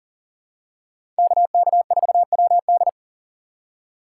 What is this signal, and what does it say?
Morse code, heard as a keyed tone, about 40 wpm, 700 Hz. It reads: KK4WD
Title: USA Call Signs - 5 Characters - 6x Character Spacing - 40wpm